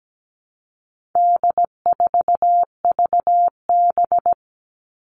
Morse code, heard as a keyed tone, about 17 wpm, 700 Hz. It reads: D4VB
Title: All Cut Numbers - 4-Digit 17wpm